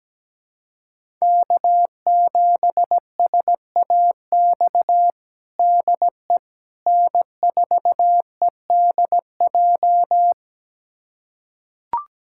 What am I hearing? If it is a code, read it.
Morse code, heard as a keyed tone, about 17 wpm, 700 Hz. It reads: K7SAX DE N4EDJ
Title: USA Callsign DE Callsign - 5 Characters - 17wpm